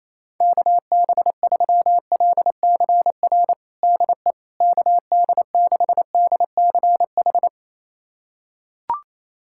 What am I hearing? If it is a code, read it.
Morse code, heard as a keyed tone, about 28 wpm, 700 Hz. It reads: KB3LCR DE KD6DCH
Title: USA Callsign DE Callsign - 6 Characters - 28wpm